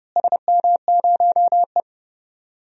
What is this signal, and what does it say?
Morse code, heard as a keyed tone, about 31 wpm, 700 Hz. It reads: SM0E
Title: DXCC-100 Call Signs - 4 Characters - Repeated 3x - 30wpm